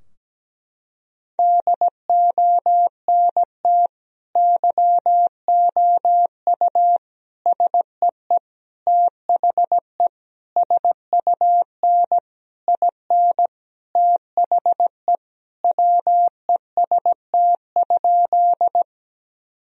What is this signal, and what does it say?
Morse code, heard as a keyed tone, about 17 wpm, 700 Hz. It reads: DONT YOU SEE THE SUN IN THE WEST?
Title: Sentences from Top 1000 Words - Review Part 2 - 17wpm